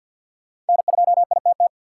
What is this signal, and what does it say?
Morse code, heard as a keyed tone, about 50 wpm, 700 Hz. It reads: N2ITT
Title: USA Call Signs - 5 Characters - 4x Character Spacing - 50wpm